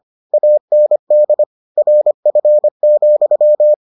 Morse code, heard as a keyed tone, about 25 wpm, 600 Hz. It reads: AND RF,